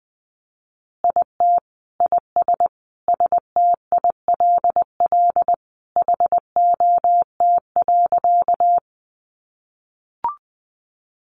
Morse code, heard as a keyed tone, about 20 wpm, 700 Hz. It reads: IT IS STILL HOT.
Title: Easy Phrases 20wpm